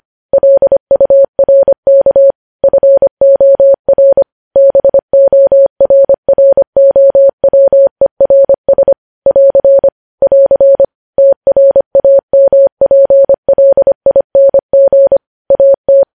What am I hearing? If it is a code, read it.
Morse code, heard as a keyed tone, about 25 wpm, 550 Hz. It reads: LURK FOR BORROWERS <AR> <AR> TRAMPLING AT
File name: podcast / News headlines-1x-25x25